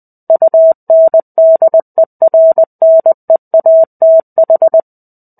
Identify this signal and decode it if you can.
Morse code, heard as a keyed tone, about 20 wpm, 650 Hz. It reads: UNDERNEATH
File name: 05-Chapter-05 20 WPM